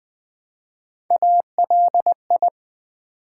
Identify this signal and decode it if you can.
Morse code, heard as a keyed tone, about 20 wpm, 700 Hz. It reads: ALI
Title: Build A Word - 1 Letter At A Time - 5-Letter Words - 20wpm